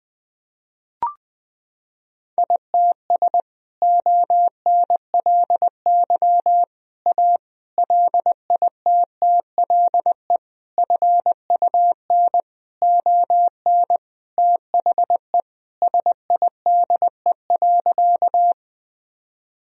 Morse code, heard as a keyed tone, about 20 wpm, 700 Hz. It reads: ITS ONLY A LITTLE FUN ON THE SIDE.
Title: Sentences from Top 1000 Words - Review Part 4 - 20wpm